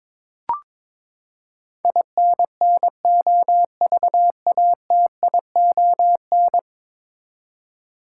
Morse code, heard as a keyed tone, about 22 wpm, 700 Hz. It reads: INNOVATION
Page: morse-code-ninja-practice-22wpm